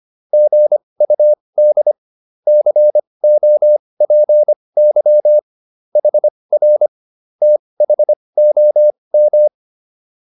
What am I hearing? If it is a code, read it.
Morse code, heard as a keyed tone, about 25 wpm, spaced as for 20 wpm, 600 Hz. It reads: GUD COPY HR THOM
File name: QSO303a 20